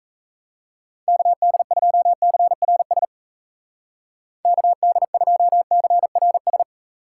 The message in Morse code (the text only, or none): KD2CRS KD2CRS